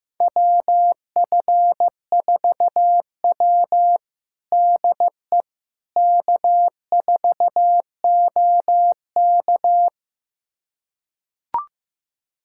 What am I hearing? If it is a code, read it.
Morse code, heard as a keyed tone, about 15 wpm, 700 Hz. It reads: WF4W DE K4OK
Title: USA Callsign DE Callsign - 4 Characters - 15wpm